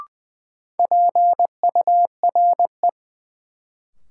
Morse code, heard as a keyed tone, about 20 wpm, 700 Hz. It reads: PURE
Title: Sentences from Top 1500 Words - New Word Drill - 20wpm